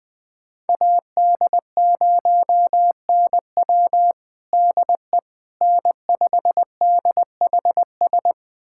Morse code, heard as a keyed tone, about 20 wpm, 700 Hz. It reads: AD0NW DE N5DHS
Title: USA Callsign DE Callsign - 5 Characters - 20wpm